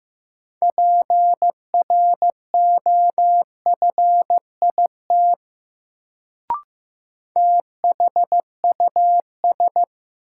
Morse code, heard as a keyed tone, about 16 wpm, 700 Hz. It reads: PROFIT THUS
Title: Sentences from Top 800 Words - New Word Drill - 15wpm